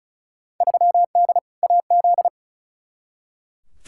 Morse code, heard as a keyed tone, about 34 wpm, 700 Hz. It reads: 3D AZ
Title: Field Day Exchanges - 35wpm